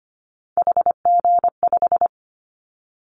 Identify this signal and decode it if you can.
Morse code, heard as a keyed tone, about 25 wpm, 700 Hz. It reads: HG5
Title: Callsign Prefixes for Top 50 DXCC - 25wpm